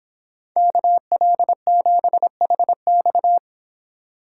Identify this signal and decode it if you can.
Morse code, heard as a keyed tone, about 26 wpm, 700 Hz. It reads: KL7HX